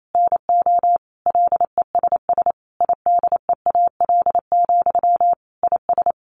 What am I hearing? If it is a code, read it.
Morse code, heard as a keyed tone, about 28 wpm, 700 Hz. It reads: NO LESS IDEAL, IS
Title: Sentences from Top 1400 Words 28wpm